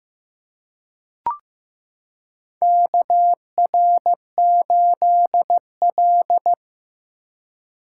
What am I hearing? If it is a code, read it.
Morse code, heard as a keyed tone, about 15 wpm, 700 Hz. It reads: KR8L
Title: K1USN SST QSO Practice - 15wpm